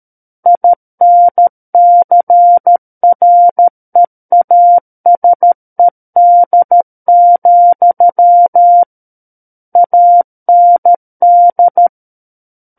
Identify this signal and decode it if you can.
Morse code, heard as a keyed tone, about 13 wpm, 700 Hz. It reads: INCREASED, AND
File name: Book One / 04-Chapter 4 - 13 WPM